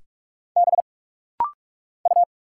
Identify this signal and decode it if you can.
Morse code, heard as a keyed tone, about 46 wpm, 690 Hz. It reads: B U
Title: Single Letters - Rapid-Fire 45wpm